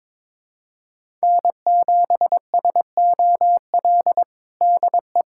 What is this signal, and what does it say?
Morse code, heard as a keyed tone, about 22 wpm, 700 Hz. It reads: N7SOL DE
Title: USA Callsign DE Callsign - 5 Characters - 22wpm